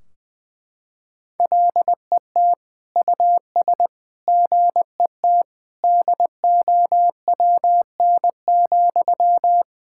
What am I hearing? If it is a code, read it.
Morse code, heard as a keyed tone, about 20 wpm, 700 Hz. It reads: LET US GET DOWN,
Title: Sentences from Top 1600 Words 20wpm